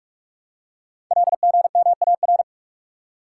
Morse code, heard as a keyed tone, about 44 wpm, 700 Hz. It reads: LGMAR
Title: Code Groups - 5 Letters - 45wpm